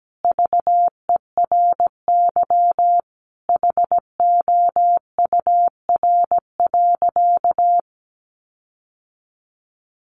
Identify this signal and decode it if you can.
Morse code, heard as a keyed tone, about 17 wpm, 700 Hz. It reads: VERY HOUR.
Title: Sentences from Top 1500 Words 17wpm